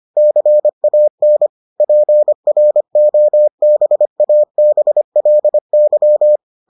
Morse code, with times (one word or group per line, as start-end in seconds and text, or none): 0.2-1.5 CAN
1.8-6.4 PROBABLY